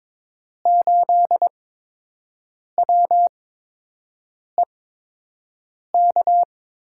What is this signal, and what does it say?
Morse code, heard as a keyed tone, about 21 wpm, spaced as for 6 wpm, 700 Hz. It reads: 8WEK
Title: USA Call Signs - 5 Characters - 8x Character Spacing - 22wpm